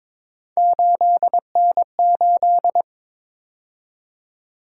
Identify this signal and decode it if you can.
Morse code, heard as a keyed tone, about 22 wpm, 700 Hz. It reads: 8N8
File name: practice / 22wpm / Three Letter-Number Combinations - 22wpm